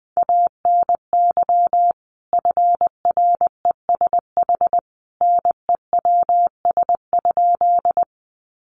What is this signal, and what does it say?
Morse code, heard as a keyed tone, about 19 wpm, 700 Hz. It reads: ANY FRESH NEWS?
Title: Sentences from Top 1200 Words 20wpm